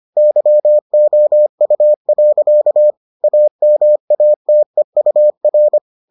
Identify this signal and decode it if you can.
Morse code, heard as a keyed tone, about 25 wpm, 600 Hz. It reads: YOU. AMATEUR